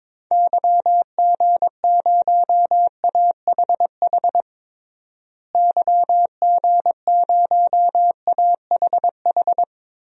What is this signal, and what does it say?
Morse code, heard as a keyed tone, about 21 wpm, 700 Hz. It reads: YG0AHH YG0AHH
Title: DXCC-100 Call Signs - 6 Characters - Repeated 3x - 22wpm